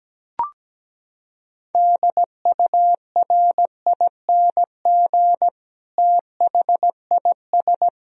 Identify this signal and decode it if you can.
Morse code, heard as a keyed tone, about 17 wpm, 700 Hz. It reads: DURING THIS
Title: Sentences from Top 1000 Words - Review Part 2 - 17wpm